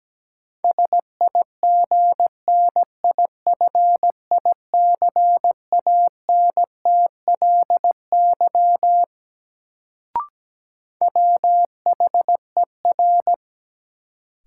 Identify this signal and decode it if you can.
Morse code, heard as a keyed tone, about 17 wpm, 700 Hz. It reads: SIGNIFICANTLY WHER
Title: Sentences from Top 1500 Words - New Word Drill Alt - 17wpm